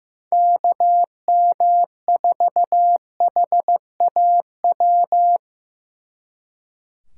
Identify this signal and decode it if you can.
Morse code, heard as a keyed tone, about 18 wpm, 700 Hz. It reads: KM4HAW
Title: Speed Racing - USA Call Signs 15wpm